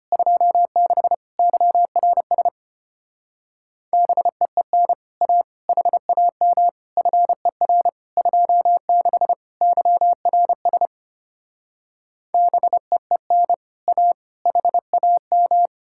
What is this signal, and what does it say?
Morse code, heard as a keyed tone, about 33 wpm, 700 Hz. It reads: 26 YRS BEEN A HAM FER 26 YRS BEEN A HAM